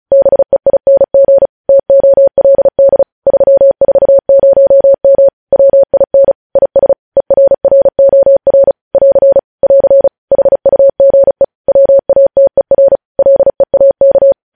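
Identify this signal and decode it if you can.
Morse code, heard as a keyed tone, about 36 wpm, 550 Hz. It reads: BEING TOLD 340M WIN IS ERROR <AR> <AR> HUGE WATER LEAK
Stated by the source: Morse news headlines sent at 35 words per minute.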